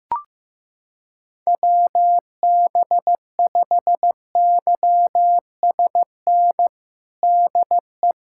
Morse code, heard as a keyed tone, about 15 wpm, 700 Hz. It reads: WB5YSN DE
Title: USA Callsign DE Callsign - 6 Characters - 15wpm